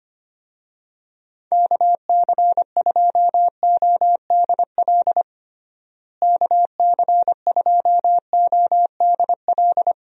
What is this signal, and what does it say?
Morse code, heard as a keyed tone, about 25 wpm, 700 Hz. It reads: KC2ODL KC2ODL
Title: USA Call Signs - 6 Characters - Repeated 2x - 25wpm